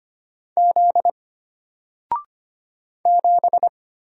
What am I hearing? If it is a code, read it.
Morse code, heard as a keyed tone, about 25 wpm, 700 Hz. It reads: Z 7
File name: Single-Letter-Number-25wpm